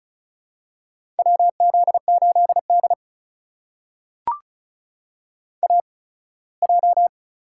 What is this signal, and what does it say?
Morse code, heard as a keyed tone, about 36 wpm, spaced as for 21 wpm, 700 Hz. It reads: WZ8D AJ
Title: USA Call Signs - 8x Character Spacing - 35wpm